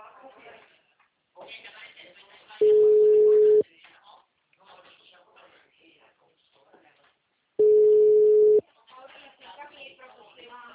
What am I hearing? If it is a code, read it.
Ringback tone (Europe, 425 Hz), with speech.